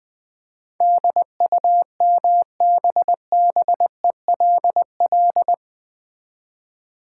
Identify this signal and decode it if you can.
Morse code, heard as a keyed tone, about 20 wpm, 700 Hz. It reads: DUMBBELL
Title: Easy Words with Hints - 20wpm